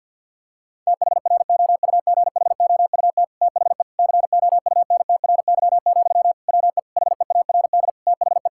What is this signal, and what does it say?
Morse code, heard as a keyed tone, about 50 wpm, 700 Hz. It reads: THROUGHOUT THE COUNTRY, WE HEARD THE
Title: Sentences from Top 1000 Words - Review Part 5 - 50wpm